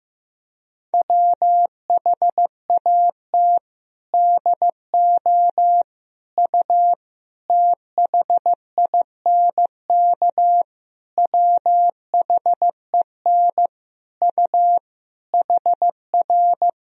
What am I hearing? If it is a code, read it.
Morse code, heard as a keyed tone, about 15 wpm, 700 Hz. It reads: WHAT DO U THINK WHEN U HR
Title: Sentences from Top 300 Words and Abbreviations 15wpm